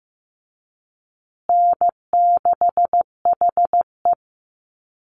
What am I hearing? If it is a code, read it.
Morse code, heard as a keyed tone, about 15 wpm, 700 Hz. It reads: N6HE
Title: Call Signs Repeated 3x - 15wpm